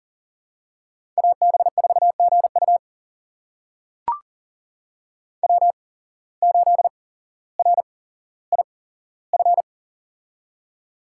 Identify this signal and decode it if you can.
Morse code, heard as a keyed tone, about 38 wpm, spaced as for 17 wpm, 700 Hz. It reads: AB4GU W8RIF
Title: USA Call Signs - 5 Characters - 8x Character Spacing - 40wpm